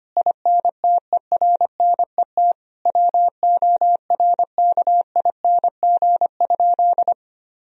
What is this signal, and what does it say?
Morse code, heard as a keyed tone, about 25 wpm, 700 Hz. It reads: INTERNET WORKING?
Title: Sentences from Top 1000 Words - Review Part 4 - 25wpm